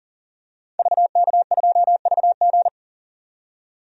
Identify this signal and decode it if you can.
Morse code, heard as a keyed tone, about 40 wpm, 700 Hz. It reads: VK2VG
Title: DXCC-100 Call Signs - 5 Characters - 40wpm